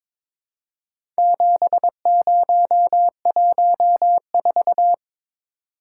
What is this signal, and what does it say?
Morse code, heard as a keyed tone, about 22 wpm, 700 Hz. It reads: 7014
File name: practice / 22wpm / Four Letter-Number Combinations - 22wpm